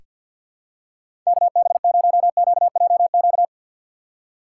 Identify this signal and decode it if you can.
Morse code, heard as a keyed tone, about 46 wpm, 700 Hz. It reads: KB0QJQ
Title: USA Call Signs - 6 Characters - 4x Character Spacing - 50wpm